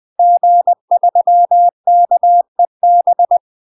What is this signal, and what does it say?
Morse code, heard as a keyed tone, about 20 wpm, 700 Hz. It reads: G3KEB
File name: UK MASTER QSOs TRAINING / 20WPM / UK MASTER QSOs 0053